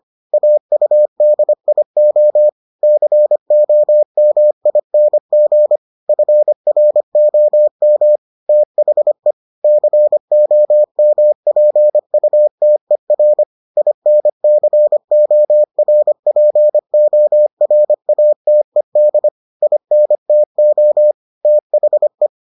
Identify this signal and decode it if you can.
Morse code, heard as a keyed tone, about 25 wpm, 600 Hz. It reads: AUDIO COMING FROM THE COMPUTER INCORPORATED INTO THE